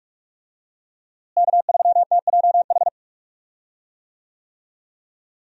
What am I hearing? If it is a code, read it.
Morse code, heard as a keyed tone, about 41 wpm, 700 Hz. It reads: K3TJH
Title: USA Call Signs - 5 Characters - Repeated 3x - 45wpm